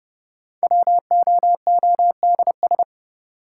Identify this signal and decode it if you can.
Morse code, heard as a keyed tone, about 30 wpm, 700 Hz. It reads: WOODS
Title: Sentences from Top 1000 Words - Review Part 2 - New Word Drill - 30wpm